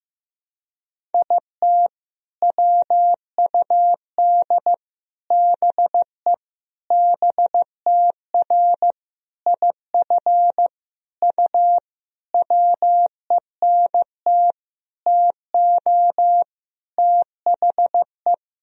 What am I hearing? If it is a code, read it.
Morse code, heard as a keyed tone, about 15 wpm, 700 Hz. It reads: IT WUD BE BTR IF U WENT TO THE
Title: Sentences from Top 400 Words and Abbreviations 15wpm